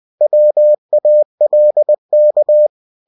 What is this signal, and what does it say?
Morse code, heard as a keyed tone, about 20 wpm, 600 Hz. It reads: WALK
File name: cw-books / winnie-the-pooh / winnie-20wpm-page-0020